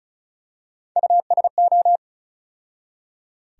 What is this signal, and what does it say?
Morse code, heard as a keyed tone, about 36 wpm, 700 Hz. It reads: USO
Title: USA Call Sign Postfixes - 3 Letters 35wpm